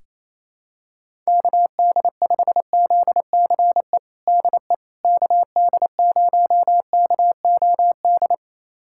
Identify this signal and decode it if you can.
Morse code, heard as a keyed tone, about 28 wpm, 700 Hz. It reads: KD5ZCE DE KD0KOD
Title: USA Callsign DE Callsign - 6 Characters - 28wpm